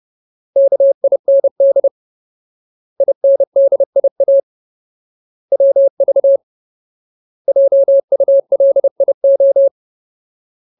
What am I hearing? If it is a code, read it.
Morse code, heard as a keyed tone, about 30 wpm, 550 Hz. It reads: KIND INDIA WV JULIO
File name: Ultimate Word List / 30WPM Ultimate